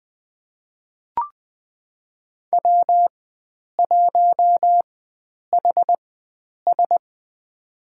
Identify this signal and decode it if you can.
Morse code, heard as a keyed tone, about 20 wpm, spaced as for 9 wpm, 700 Hz. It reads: W1HS
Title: USA Call Signs - 4x Character Spacing - 20wpm